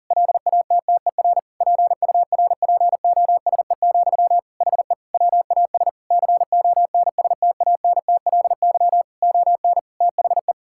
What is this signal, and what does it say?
Morse code, heard as a keyed tone, about 40 wpm, 700 Hz. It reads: LATTER PURPOSE, HE WAS CONSTANTLY ON THE
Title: Sentences from Top 1600 Words 40wpm